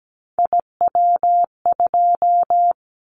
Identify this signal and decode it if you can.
Morse code, heard as a keyed tone, about 17 wpm, 700 Hz. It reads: IW2
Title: Callsign Prefixes for Top 25 DXCC - 17wpm